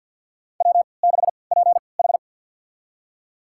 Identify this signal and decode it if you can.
Morse code, heard as a keyed tone, about 53 wpm, spaced as for 29 wpm, 700 Hz. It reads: W6PH
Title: USA Call Signs - 4 Characters - 3x Character Spacing - 50wpm